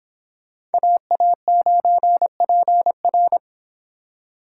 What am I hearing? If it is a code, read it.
Morse code, heard as a keyed tone, about 29 wpm, 700 Hz. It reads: AA9PR